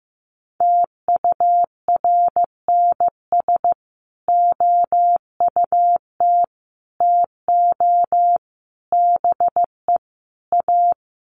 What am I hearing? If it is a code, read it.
Morse code, heard as a keyed tone, about 15 wpm, 700 Hz. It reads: TURNS OUT TO BE A
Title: Sets of 5 Words Encore 15wpm